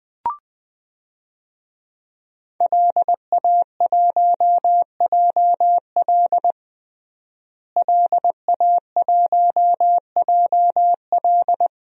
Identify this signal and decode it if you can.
Morse code, heard as a keyed tone, about 21 wpm, 700 Hz. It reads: LA1JL LA1JL
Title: DXCC-100 Call Signs - 5 Characters - Repeated 2x - 20wpm